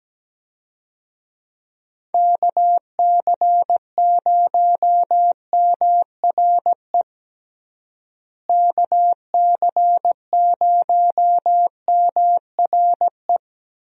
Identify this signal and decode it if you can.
Morse code, heard as a keyed tone, about 17 wpm, 700 Hz. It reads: KC0MRE KC0MRE
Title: USA Call Signs - 6 Characters - Repeated 2x - 17wpm